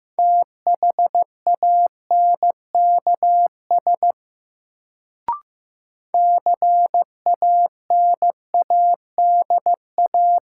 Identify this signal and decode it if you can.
Morse code, heard as a keyed tone, about 15 wpm, 700 Hz. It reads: THANKS CANADA
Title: Sets of 5 Words - New Word Drill - 15wpm